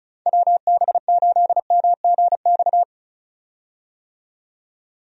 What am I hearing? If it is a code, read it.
Morse code, heard as a keyed tone, about 36 wpm, 700 Hz. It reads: WB8MGX
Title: USA Call Signs - 6 Characters - 35wpm